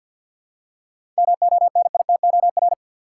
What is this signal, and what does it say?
Morse code, heard as a keyed tone, about 51 wpm, 700 Hz. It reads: MONITOR
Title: Seven Letter Words - 50wpm